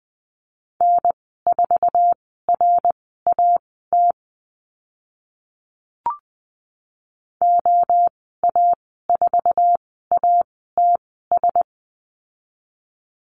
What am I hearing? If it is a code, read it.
Morse code, heard as a keyed tone, about 20 wpm, spaced as for 15 wpm, 700 Hz. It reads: N4RAT OA4ATS
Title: 062 Lesson 11g - Letter h - Call Signs 20wpm